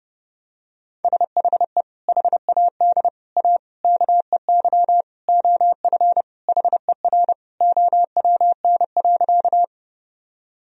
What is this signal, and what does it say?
Morse code, heard as a keyed tone, about 28 wpm, 700 Hz. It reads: SHE HAD A KEY OF HER OWN.
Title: Sentences from Top 1000 Words - Review Part 4 - 30wpm